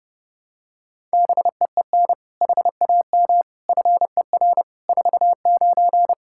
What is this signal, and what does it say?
Morse code, heard as a keyed tone, about 31 wpm, 700 Hz. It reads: BEEN HAM FER 49
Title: K1USN SST QSO Practice 30wpm